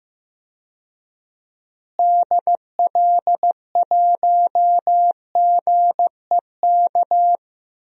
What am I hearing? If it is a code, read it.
Morse code, heard as a keyed tone, about 15 wpm, 700 Hz. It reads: DL1GEK
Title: DXCC-100 Call Signs - 6 Characters - Repeated 2x - 15wpm